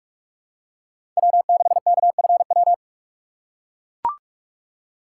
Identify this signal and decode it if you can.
Morse code, heard as a keyed tone, about 41 wpm, 700 Hz. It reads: W6KFW